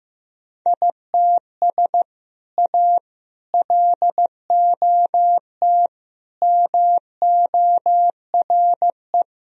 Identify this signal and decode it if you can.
Morse code, heard as a keyed tone, about 16 wpm, 700 Hz. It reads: ITS A LOT MORE
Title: Sentences from Top 1000 Words - Review Part 3 - 15wpm